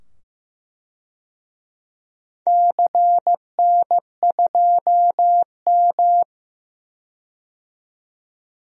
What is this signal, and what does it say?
Morse code, heard as a keyed tone, about 15 wpm, 700 Hz. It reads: CN2M
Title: DXCC-100 Call Signs - 4 Characters - 15wpm